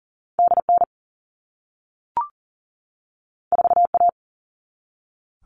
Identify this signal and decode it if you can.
Morse code, heard as a keyed tone, about 40 wpm, 700 Hz. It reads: DN 4A
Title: All Cut Numbers - 2-Digit 40wpm